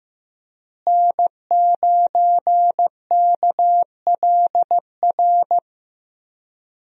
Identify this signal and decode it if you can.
Morse code, heard as a keyed tone, about 15 wpm, 700 Hz. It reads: N9KLR
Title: USA Call Signs - 5 Characters - Repeated 3x - 15wpm